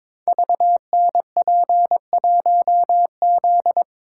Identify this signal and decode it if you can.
Morse code, heard as a keyed tone, about 21 wpm, 700 Hz. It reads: VNP1Z
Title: Code Groups - 5 Letter:Numbers - 22wpm